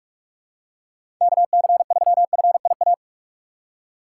Morse code, heard as a keyed tone, about 44 wpm, 700 Hz. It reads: KC3FIA
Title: USA Call Signs - 6 Characters - 6x Character Spacing - 45wpm